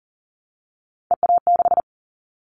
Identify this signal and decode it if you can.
Morse code, heard as a keyed tone, about 40 wpm, 700 Hz. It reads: EA6
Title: Callsign Prefixes for Top 100 DXCC - 40wpm